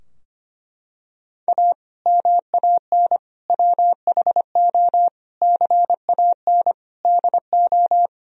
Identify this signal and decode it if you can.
Morse code, heard as a keyed tone, about 25 wpm, 700 Hz. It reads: A MAN WHO CAN DO
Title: Sentences from Top 1000 Words - Review Part 2 - 25wpm